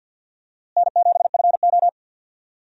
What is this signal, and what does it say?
Morse code, heard as a keyed tone, about 52 wpm, 700 Hz. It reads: N7FO
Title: USA Call Signs - 4 Characters - Repeated 3x - 50wpm